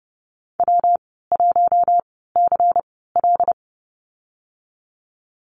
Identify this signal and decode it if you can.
Morse code, heard as a keyed tone, about 30 wpm, spaced as for 17 wpm, 700 Hz. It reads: W1CL
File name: course / 30:10wpm / 116 Lesson 19g - Letter w - Call Signs 30wpm